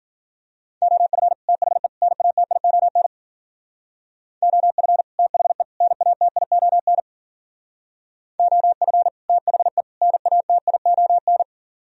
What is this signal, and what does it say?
Morse code, heard as a keyed tone, about 49 wpm, 700 Hz. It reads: OF THE NATION OF THE NATION OF THE NATION
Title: Speed-Racing - Sets of 3 Words 40wpm